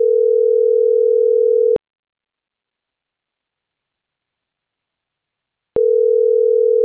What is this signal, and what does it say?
Ringback tone (North America), 440 + 480 Hz.